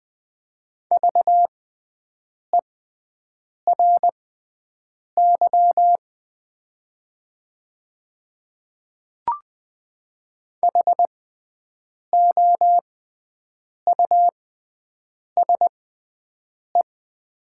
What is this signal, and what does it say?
Morse code, heard as a keyed tone, about 20 wpm, spaced as for 7 wpm, 700 Hz. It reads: VERY HOUSE
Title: 6x Character Spacing - Top 300 Words - 20wpm